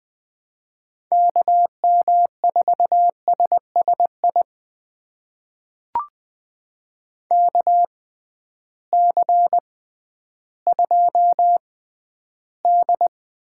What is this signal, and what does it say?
Morse code, heard as a keyed tone, about 20 wpm, spaced as for 12 wpm, 700 Hz. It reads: KM4SSI KC2D
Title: USA Call Signs - 6 Characters - 6x Character Spacing - 20wpm